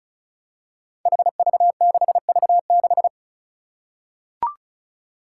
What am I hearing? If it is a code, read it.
Morse code, heard as a keyed tone, about 34 wpm, 700 Hz. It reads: HV6V6
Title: Five Challenging Letter-Number Combinations 35wpm